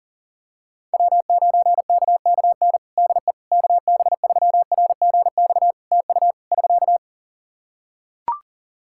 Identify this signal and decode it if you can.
Morse code, heard as a keyed tone, about 40 wpm, 700 Hz. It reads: W9KKN DE KB3RGX TU <SK>
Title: Practice QSO - Exchange 4 - Ending - 40wpm